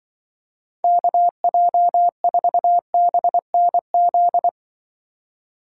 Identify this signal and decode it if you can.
Morse code, heard as a keyed tone, about 25 wpm, 700 Hz. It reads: KJ4BNZ
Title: Speed Racing - USA 2x3 Call Signs 20wpm